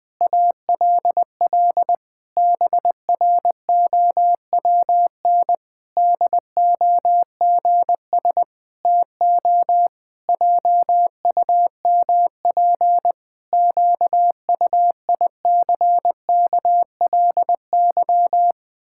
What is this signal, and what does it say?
Morse code, heard as a keyed tone, about 21 wpm, 700 Hz. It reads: ALL BROWN DOGS TO JUMP QUICKLY
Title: Pangram Challenge v2 - 20wpm